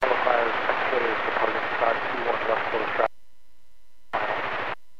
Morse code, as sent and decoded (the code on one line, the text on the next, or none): none